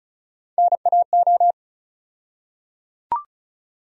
Morse code, heard as a keyed tone, about 36 wpm, 700 Hz. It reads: NAO
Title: USA Call Sign Postfixes - 3 Letters 35wpm